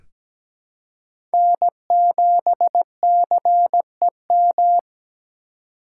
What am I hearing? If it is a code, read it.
Morse code, heard as a keyed tone, about 17 wpm, 700 Hz. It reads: N7CEM
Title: USA Call Signs - 5 Characters - 3x Character Spacing - 17wpm